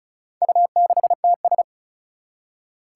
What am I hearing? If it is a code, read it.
Morse code, heard as a keyed tone, about 36 wpm, 700 Hz. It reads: U6TS
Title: Four Challenging Letter-Number Combinations 35wpm